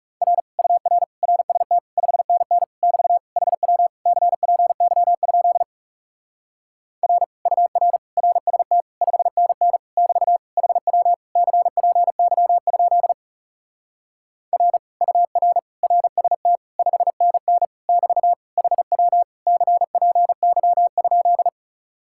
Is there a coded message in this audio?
Yes: Morse code from 0.2 to 21.5 s, heard as a keyed tone, about 40 wpm, 700 Hz. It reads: R UR RST 5NN <BT> HW CPY? R UR RST 5NN <BT> HW CPY? R UR RST 5NN <BT> HW CPY?